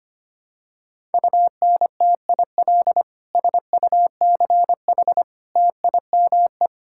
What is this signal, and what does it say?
Morse code, heard as a keyed tone, about 25 wpm, 700 Hz. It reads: UNTIL SUCH TIME
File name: practice / 25wpm / Sentences with 9-Word Phrases - 25wpm